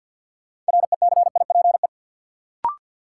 Morse code, heard as a keyed tone, about 49 wpm, 700 Hz. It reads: RECIPE
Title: Sentences from Top 1200 Words - New Word Drill - 50wpm